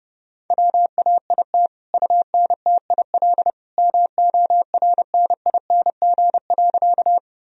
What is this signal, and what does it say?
Morse code, heard as a keyed tone, about 30 wpm, 700 Hz. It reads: WAIT UNTIL MORNING.
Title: Sentences from Top 1000 Words - Review Part 1 - 30wpm